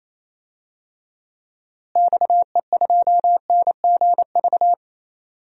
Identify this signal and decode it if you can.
Morse code, heard as a keyed tone, about 28 wpm, 700 Hz. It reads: XE2NGV
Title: DXCC-100 Call Signs - 6 Characters - Repeated 3x - 28wpm